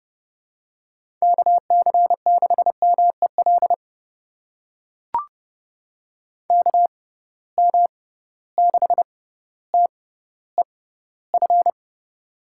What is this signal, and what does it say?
Morse code, heard as a keyed tone, about 32 wpm, spaced as for 16 wpm, 700 Hz. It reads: KC6MEL KM6TEF
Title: USA Call Signs - 6 Characters - 6x Character Spacing - 30wpm